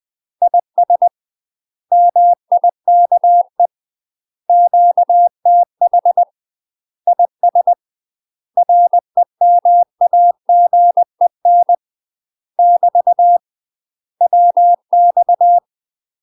Morse code, heard as a keyed tone, about 20 wpm, 700 Hz. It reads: IS MIKE QTH IS REMAGEN <BT> WX
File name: UK MASTER QSOs TRAINING / 20WPM / UK MASTER QSOs 0041